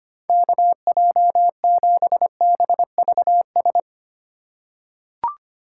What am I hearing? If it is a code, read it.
Morse code, heard as a keyed tone, about 25 wpm, 700 Hz. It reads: KJ7BVS
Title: USA Call Signs - 3x Character Spacing - 25wpm